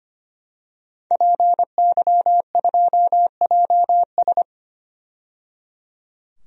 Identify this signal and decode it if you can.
Morse code, heard as a keyed tone, about 25 wpm, 700 Hz. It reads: PY2JS
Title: DXCC-100 Call Signs - 5 Characters - 25wpm